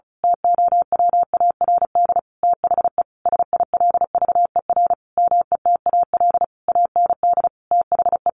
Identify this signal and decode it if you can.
Morse code, heard as a keyed tone, about 34 wpm, 700 Hz. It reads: TOWARD THE SILVER METAL AND THE
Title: A Sound Of Thunder 35wpm